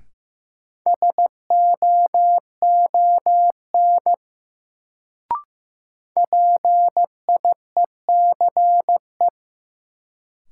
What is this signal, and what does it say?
Morse code, heard as a keyed tone, about 15 wpm, 700 Hz. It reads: SOON PIECE
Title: Sentences from Top 400 Words - New Word Drill - 15wpm